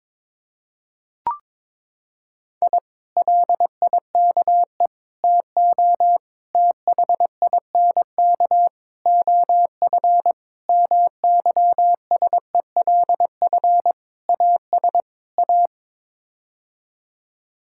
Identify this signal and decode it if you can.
Morse code, heard as a keyed tone, about 22 wpm, 700 Hz. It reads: I LIKE TO THINK OF MYSELF AS A
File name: practice / 22wpm / Sets of 8 Words - 22wpm